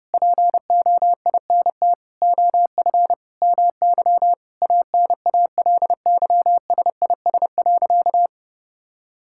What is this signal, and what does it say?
Morse code, heard as a keyed tone, about 32 wpm, 700 Hz. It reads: POINT OF MY ANALYSIS.
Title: Sentences from Top 1000 Words - Review Part 5 - 30wpm